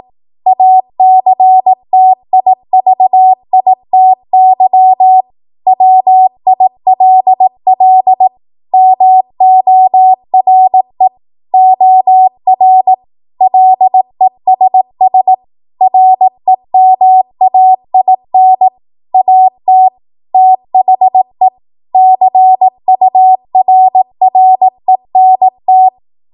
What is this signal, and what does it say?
Morse code, heard as a keyed tone, about 18 wpm, 750 Hz. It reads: ACTIVITY WILL MORE OR LESS REMAIN AT THE CURRENT
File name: Morse / 18WPM